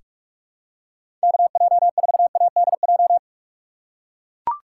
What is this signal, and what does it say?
Morse code, heard as a keyed tone, about 43 wpm, 700 Hz. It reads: KJ4ADJ
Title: USA Call Signs - 6 Characters - 6x Character Spacing - 45wpm